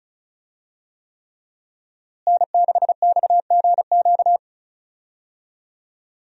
Morse code, heard as a keyed tone, about 36 wpm, 700 Hz. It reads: N6XGQ
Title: USA Call Signs - 5 Characters - Repeated 3x - 35wpm